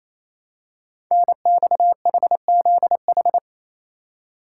Morse code, heard as a keyed tone, about 27 wpm, 700 Hz. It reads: NXHZH
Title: Code Groups - 5 Letters - 28wpm